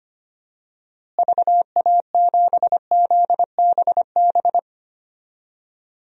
Morse code, heard as a keyed tone, about 25 wpm, 700 Hz. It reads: VA7ZBB
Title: DXCC-100 Call Signs - 6 Characters - Repeated 2x - 25wpm